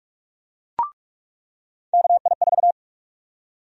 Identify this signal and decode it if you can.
Morse code, heard as a keyed tone, about 45 wpm, 700 Hz. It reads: KI4
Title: USA Call Sign Prefixes - 45wpm